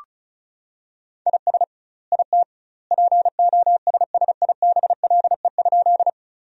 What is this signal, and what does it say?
Morse code, heard as a keyed tone, about 34 wpm, 700 Hz. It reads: IS IT POSSIBLE?
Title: 2x Word-Spacing - Sentences from Top 500 Words - 35wpm